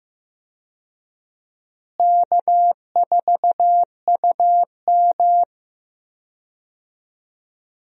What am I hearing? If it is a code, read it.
Morse code, heard as a keyed tone, about 15 wpm, 700 Hz. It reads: K4UM
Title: USA Call Signs - 4 Characters - Repeated 3x - 15wpm